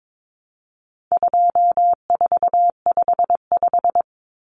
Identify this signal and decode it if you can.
Morse code, heard as a keyed tone, about 22 wpm, 700 Hz. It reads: 2455
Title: Common Cut Numbers - 4-Digit 22wpm